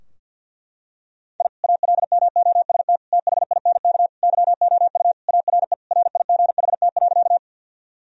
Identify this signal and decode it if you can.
Morse code, heard as a keyed tone, about 50 wpm, 700 Hz. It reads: I ALMOST THINK YOU ARE RIGHT.
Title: Sentences from Top 1000 Words - Review Part 4 - 50wpm